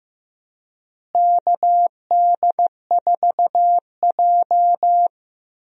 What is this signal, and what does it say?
Morse code, heard as a keyed tone, about 16 wpm, 700 Hz. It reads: KD4J
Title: USA Call Signs - 4 Characters - Repeated 2x - 15wpm